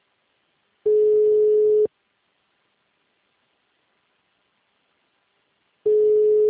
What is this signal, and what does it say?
Ringback tone (Europe, 425 Hz).